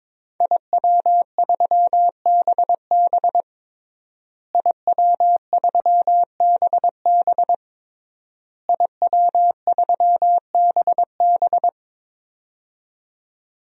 Morse code, heard as a keyed tone, about 22 wpm, 700 Hz. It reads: IW3BB IW3BB IW3BB
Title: DXCC-100 Call Signs - 5 Characters - Repeated 3x - 22wpm